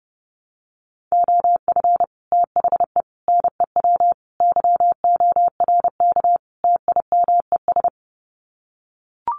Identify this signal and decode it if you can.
Morse code, heard as a keyed tone, about 30 wpm, 700 Hz. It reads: OF THE NEW YORK TIMES
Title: Sets of 5 Words 30wpm